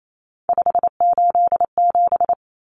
Morse code, heard as a keyed tone, about 28 wpm, 700 Hz. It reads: 587
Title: Numbers - 3-Digit 28wpm